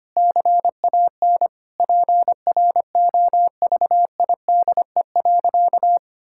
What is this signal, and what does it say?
Morse code, heard as a keyed tone, about 25 wpm, 700 Hz. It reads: CAN PROVIDE.
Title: Sentences from Top 1000 Words - Review Part 6 - 25wpm